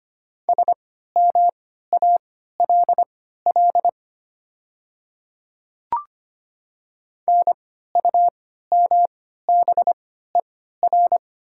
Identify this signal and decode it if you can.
Morse code, heard as a keyed tone, about 25 wpm, spaced as for 14 wpm, 700 Hz. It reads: SMALL NUMBER
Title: 3x Character Spacing - Top 200 Words - 25wpm